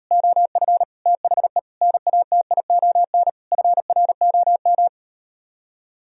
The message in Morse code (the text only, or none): OF THE NATION FROM